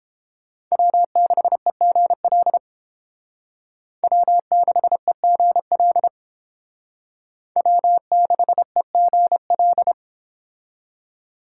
Morse code, heard as a keyed tone, about 29 wpm, 700 Hz. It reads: W6EGL W6EGL W6EGL